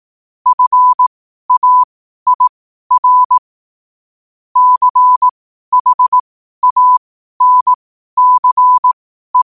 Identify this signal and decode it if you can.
Morse code, heard as a keyed tone, about 18 wpm, spaced as for 12 wpm, 1000 Hz. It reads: FAIR CHANCE
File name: qotdfiles / 12wpm